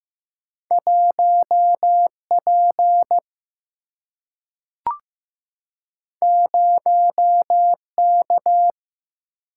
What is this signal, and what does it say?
Morse code, heard as a keyed tone, about 15 wpm, 700 Hz. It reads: 1P 0K
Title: Single-Letter Single-Number 15wpm